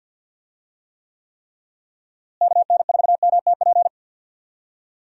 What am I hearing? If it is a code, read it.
Morse code, heard as a keyed tone, about 52 wpm, 700 Hz. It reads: KN4MTP
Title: USA Call Signs - 6 Characters - Repeated 3x - 50wpm